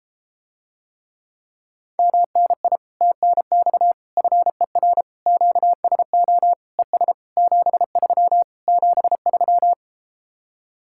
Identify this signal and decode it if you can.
Morse code, heard as a keyed tone, about 34 wpm, 700 Hz. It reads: MNI TNX FER QSO ES 73 73